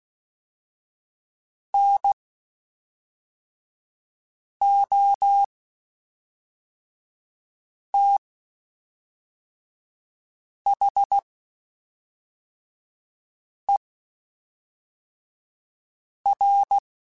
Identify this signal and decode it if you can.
Morse code, heard as a keyed tone, about 16 wpm, spaced as for 3 wpm, 800 Hz. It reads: NOTHER
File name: (spaced) / QSOs Spaced-mp3s / WA7QOX 1 @5 sp